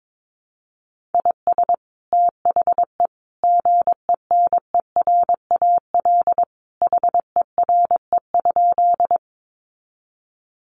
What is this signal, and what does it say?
Morse code, heard as a keyed tone, about 22 wpm, 700 Hz. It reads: IS THE GENERAL HERE?
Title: Sentences from Top 600 Words 22wpm